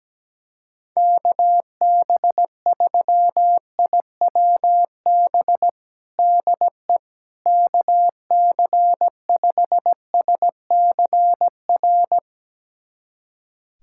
Morse code, heard as a keyed tone, about 17 wpm, 700 Hz. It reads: KB3IWB DE KC5SCR
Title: USA Callsign DE Callsign - 6 Characters - 17wpm